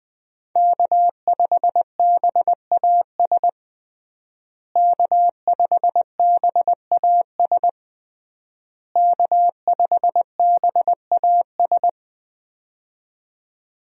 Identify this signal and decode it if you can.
Morse code, heard as a keyed tone, about 20 wpm, 700 Hz. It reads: K5BAS K5BAS K5BAS
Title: USA Call Signs - 5 Characters - Repeated 3x - 20wpm